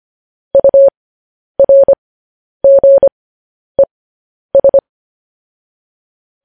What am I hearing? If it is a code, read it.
Morse code, heard as a keyed tone, about 25 wpm, spaced as for 10 wpm, 550 Hz. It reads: URGES
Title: News Headlines at 10